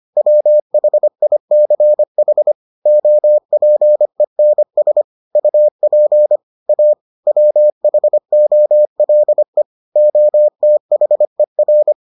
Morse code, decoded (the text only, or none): WHICH OPENS UP A WHOLE OTHER